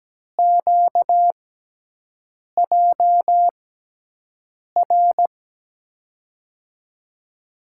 Morse code, heard as a keyed tone, about 17 wpm, spaced as for 6 wpm, 700 Hz. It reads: QJR
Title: USA Call Signs - 5 Characters - 6x Character Spacing - 17wpm